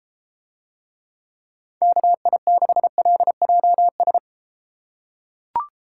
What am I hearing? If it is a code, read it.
Morse code, heard as a keyed tone, about 33 wpm, 700 Hz. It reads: KI6LJS